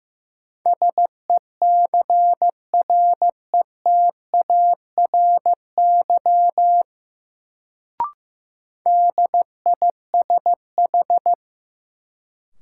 Morse code, heard as a keyed tone, about 15 wpm, 700 Hz. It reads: SECRETARY DISH
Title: Sentences from Top 1300 Words - New Word Drill Alt - 15wpm